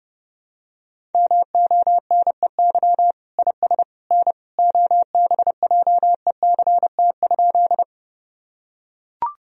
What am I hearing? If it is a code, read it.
Morse code, heard as a keyed tone, about 28 wpm, 700 Hz. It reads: MONEY IS N OBJECT?
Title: Sentences from Top 500 Words and Abbreviations 30wpm